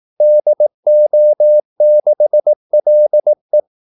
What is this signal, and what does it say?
Morse code, heard as a keyed tone, about 18 wpm, 600 Hz. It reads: DO6LE
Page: Kaldesignaler 18 wpm